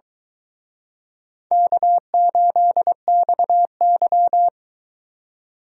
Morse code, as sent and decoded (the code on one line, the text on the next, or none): -.- ---.. -..- -.--
K8XY